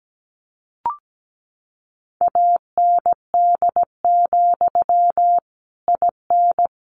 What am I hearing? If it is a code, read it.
Morse code, heard as a keyed tone, about 17 wpm, 700 Hz. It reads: AND, IN
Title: Sentences from Top 1200 Words 17wpm